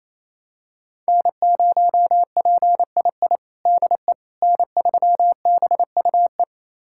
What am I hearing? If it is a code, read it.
Morse code, heard as a keyed tone, about 28 wpm, 700 Hz. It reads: N0PII DE N3BUE
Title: USA Callsign DE Callsign - 5 Characters - 28wpm